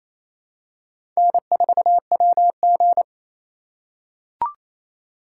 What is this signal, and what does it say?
Morse code, heard as a keyed tone, about 28 wpm, 700 Hz. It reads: N4WG
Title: USA Call Signs - 4x Character Spacing - 28wpm